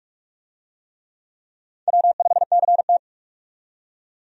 Morse code, heard as a keyed tone, about 44 wpm, 700 Hz. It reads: W5CT
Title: USA Call Signs - 4 Characters - 45wpm